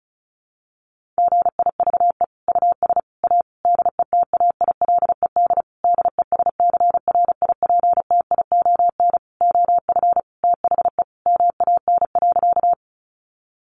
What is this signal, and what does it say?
Morse code, heard as a keyed tone, about 34 wpm, 700 Hz. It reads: GIVE US A DETAILED DESCRIPTION OF THE MAN.
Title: Sentences from Top 1500 Words 35wpm